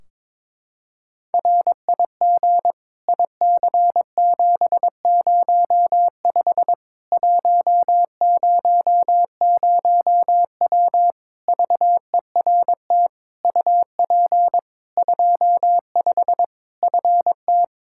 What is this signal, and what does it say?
Morse code, heard as a keyed tone, about 22 wpm, 700 Hz. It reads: RIG IC705 100W VERT UP 25 FT
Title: K1USN SST QSO Practice 22wpm